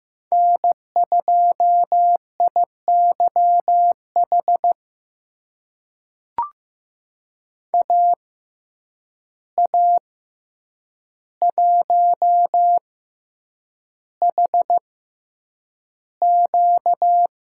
Morse code, heard as a keyed tone, about 15 wpm, spaced as for 8 wpm, 700 Hz. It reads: N2IYH AA1HQ
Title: USA Call Signs - 5 Characters - 6x Character Spacing - 15wpm